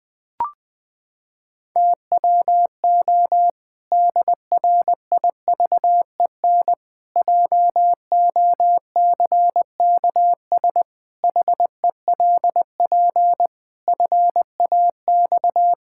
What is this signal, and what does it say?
Morse code, heard as a keyed tone, about 20 wpm, 700 Hz. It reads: TWO DRIVEN JOCKS HELP FAX
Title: Pangram Challenge v2 - 20wpm